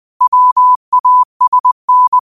Morse code, heard as a keyed tone, about 20 wpm, 1000 Hz. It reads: WASN
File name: qotdfiles / 20wpm